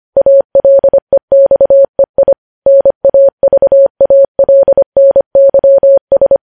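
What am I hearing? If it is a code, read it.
Morse code, heard as a keyed tone, about 25 wpm, 550 Hz. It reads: ALEXEI NAVALNYS
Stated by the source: Morse code news headlines at 25 words per minute.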